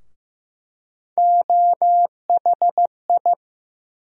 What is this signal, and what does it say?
Morse code, heard as a keyed tone, about 15 wpm, 700 Hz. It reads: OHI
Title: USA Call Sign Postfixes - 3 Letters 15wpm